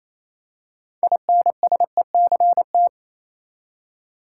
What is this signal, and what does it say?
Morse code, heard as a keyed tone, about 27 wpm, 700 Hz. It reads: INSECT
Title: Six Letter Words - 28wpm